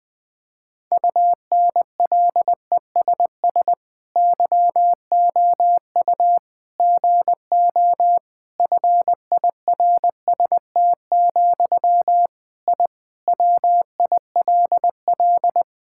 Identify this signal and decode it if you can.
Morse code, heard as a keyed tone, about 20 wpm, 700 Hz. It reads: UNLESS YOU GO FIRST, I WILL
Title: Sentences from Top 1000 Words - Review Part 6 - 20wpm